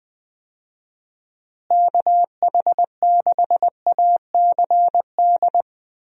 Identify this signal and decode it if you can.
Morse code, heard as a keyed tone, about 20 wpm, 700 Hz. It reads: KH6ACD
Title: USA Call Signs - 6 Characters - Repeated 3x - 20wpm